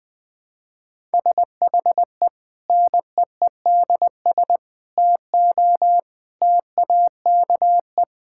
Morse code, heard as a keyed tone, about 20 wpm, 700 Hz. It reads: SHE NEEDS TO TAKE
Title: Sentences with 4-Word Phrases - Spoken Beginning - 20wpm